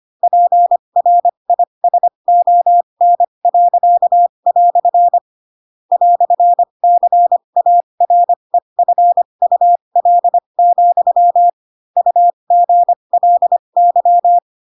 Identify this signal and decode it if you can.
Morse code, heard as a keyed tone, about 25 wpm, spaced as for 22 wpm, 700 Hz. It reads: PRISON." "CAREFUL, UGLY
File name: Pinocchio / Pinocchio ch4 18 25 ik1hge